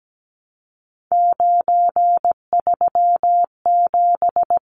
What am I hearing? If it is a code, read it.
Morse code, heard as a keyed tone, about 17 wpm, 700 Hz. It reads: 937
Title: Numbers - 3-Digit 17wpm